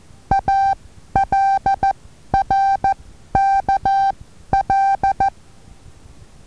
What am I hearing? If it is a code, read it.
Morse code, heard as a keyed tone, about 14 wpm, spaced as for 11 wpm, 780 Hz. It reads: ALRKL
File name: Taviro HA6VA B